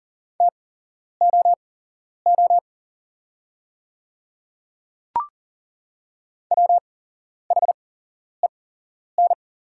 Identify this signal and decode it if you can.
Morse code, heard as a keyed tone, about 40 wpm, spaced as for 11 wpm, 700 Hz. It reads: TOO WHEN
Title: 8x Character Spacing - Top 200 Words - 40wpm